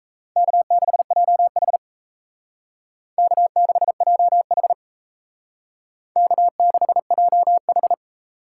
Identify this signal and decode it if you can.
Morse code, heard as a keyed tone, about 36 wpm, 700 Hz. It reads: K6JH K6JH K6JH